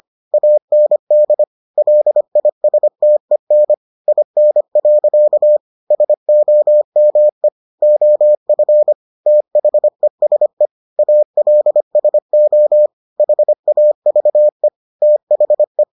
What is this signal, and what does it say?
Morse code, heard as a keyed tone, about 25 wpm, 600 Hz. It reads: AND LISTEN IN. SOME OF THESE ALSO HAVE THE